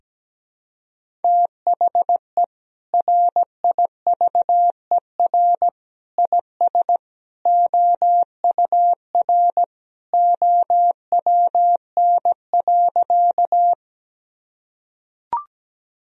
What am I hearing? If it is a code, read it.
Morse code, heard as a keyed tone, about 17 wpm, 700 Hz. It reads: THE RIVER IS OUR OWN.
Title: Sentences from Top 1000 Words - Review Part 1 - 17wpm